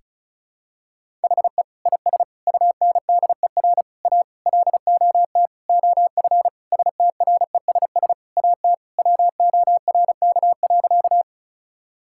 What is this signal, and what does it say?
Morse code, heard as a keyed tone, about 36 wpm, 700 Hz. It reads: HE IS UNDER A LOT OF STRESS AT WORK.
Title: Sentences from Top 1000 Words - Review Part 6 - 35wpm